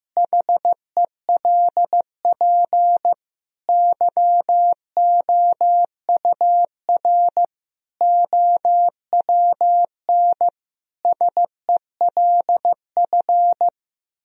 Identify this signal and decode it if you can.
Morse code, heard as a keyed tone, about 15 wpm, 700 Hz. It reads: HELP YOUR OWN SELF
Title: Sentences from Top 1000 Words - Review Part 1 - 15wpm